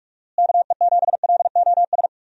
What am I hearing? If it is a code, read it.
Morse code, heard as a keyed tone, about 46 wpm, 700 Hz. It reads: KE7LOS
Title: USA Call Signs - 6 Characters - Repeated 3x - 45wpm